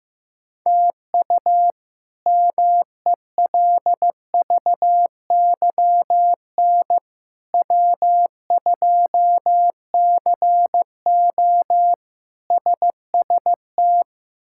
Morse code, heard as a keyed tone, about 15 wpm, 700 Hz. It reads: TU MELVYN W2CO SST
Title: K1USN SST QSO Practice - 15wpm